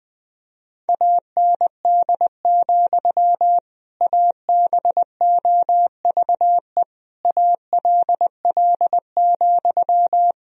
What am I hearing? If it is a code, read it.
Morse code, heard as a keyed tone, about 20 wpm, 700 Hz. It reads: AND, ABOVE ALL,
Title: Sentences from Top 1000 Words - Review Part 2 - 20wpm